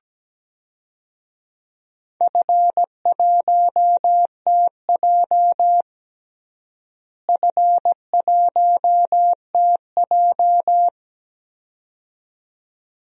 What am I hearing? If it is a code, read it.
Morse code, heard as a keyed tone, about 17 wpm, 700 Hz. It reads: F1TJ F1TJ
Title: DXCC-100 Call Signs - 4 Characters - Repeated 2x - 17wpm